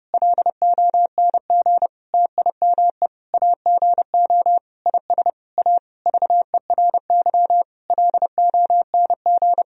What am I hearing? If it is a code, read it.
Morse code, heard as a keyed tone, about 30 wpm, 700 Hz. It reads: LONG TIME AGO IS A VERY LONG
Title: Sentences from Top 1000 Words - Review Part 3 - 30wpm